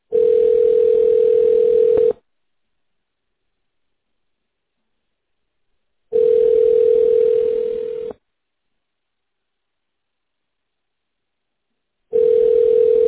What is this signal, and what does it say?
Ringback tone (North America), 440 + 480 Hz.